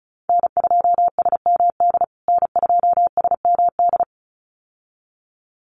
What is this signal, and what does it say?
Morse code, heard as a keyed tone, about 34 wpm, 700 Hz. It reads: N2SMD N2SMD
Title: USA Call Signs Repeated 2x 35wpm